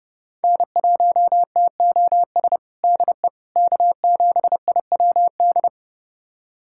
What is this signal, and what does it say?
Morse code, heard as a keyed tone, about 30 wpm, 700 Hz. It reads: N1TOS DE K7IWD
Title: USA Callsign DE Callsign - 5 Characters - 30wpm